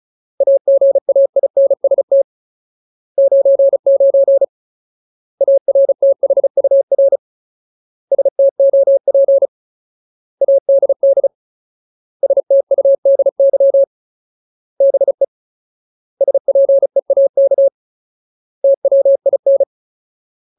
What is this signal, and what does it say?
Morse code, heard as a keyed tone, about 34 wpm, 550 Hz. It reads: AGAINST 99 ARTHUR STOP ADD STUDY BE SPEAK TWIN
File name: Ultimate Word List / 35WPM Ultimate